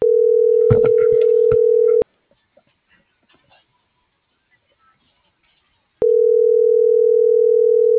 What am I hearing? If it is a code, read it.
Ringback tone (North America), 440 + 480 Hz.